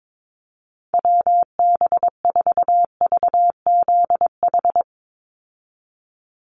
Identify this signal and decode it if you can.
Morse code, heard as a keyed tone, about 22 wpm, 700 Hz. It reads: WB4VZH
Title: USA Call Signs 22wpm